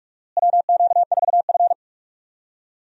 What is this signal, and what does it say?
Morse code, heard as a keyed tone, about 46 wpm, 700 Hz. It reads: WQ4F
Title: USA Call Signs - 4 Characters - Repeated 2x - 45wpm